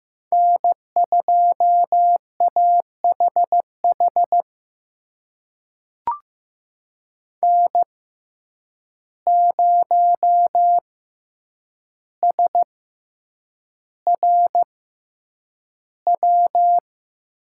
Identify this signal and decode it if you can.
Morse code, heard as a keyed tone, about 15 wpm, spaced as for 8 wpm, 700 Hz. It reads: N2AHH N0SRW
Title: USA Call Signs - 5 Characters - 6x Character Spacing - 15wpm